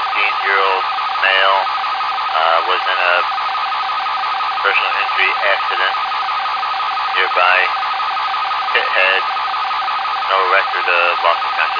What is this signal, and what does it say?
Two-tone teleprinter signal, about 450 Hz shift, with speech.